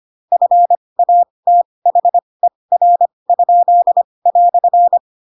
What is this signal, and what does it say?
Morse code, heard as a keyed tone, about 25 wpm, spaced as for 20 wpm, 700 Hz. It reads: FATHER?"
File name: Pinocchio / Pinocchio ch8 15 25 ik1hge